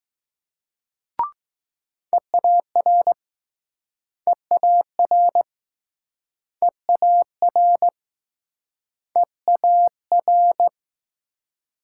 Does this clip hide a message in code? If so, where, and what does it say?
Yes: Morse code from 2.1 to 10.7 s, heard as a keyed tone, about 20 wpm, 700 Hz. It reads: EAR EAR EAR EAR